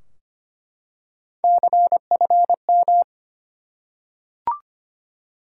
Morse code, heard as a keyed tone, about 25 wpm, 700 Hz. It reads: CFM
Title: Top 90 QSO Elements 25wpm